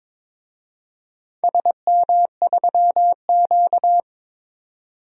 Morse code, heard as a keyed tone, about 22 wpm, 700 Hz. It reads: SM3Q
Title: DXCC-100 Call Signs - 4 Characters - 22wpm